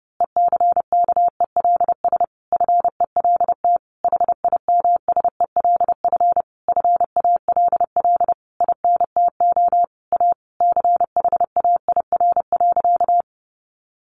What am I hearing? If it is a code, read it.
Morse code, heard as a keyed tone, about 30 wpm, 700 Hz. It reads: ECKELS FELT HIMSELF FALL INTO A CHAIR.
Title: A Sound Of Thunder 30wpm